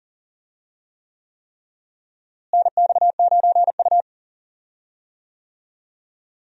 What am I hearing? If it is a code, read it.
Morse code, heard as a keyed tone, about 40 wpm, 700 Hz. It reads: NX9U
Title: USA Call Signs - 4 Characters - Repeated 3x - 40wpm